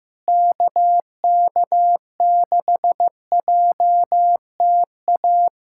Morse code, heard as a keyed tone, about 15 wpm, 700 Hz. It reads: KK6JTA
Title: USA Callsign DE Callsign - 6 Characters - 15wpm